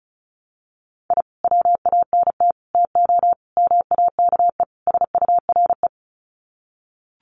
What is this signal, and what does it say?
Morse code, heard as a keyed tone, about 34 wpm, 700 Hz. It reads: I WANT TO MAKE SURE
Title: Sets of 5 Words Encore2 35wpm